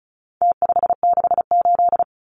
Morse code, heard as a keyed tone, about 34 wpm, 700 Hz. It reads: T568
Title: Numbers - Contest Style - 4-Digit 35wpm